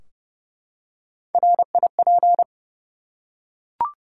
Morse code, heard as a keyed tone, about 28 wpm, 700 Hz. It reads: RIP
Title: Three Letter Words - 30wpm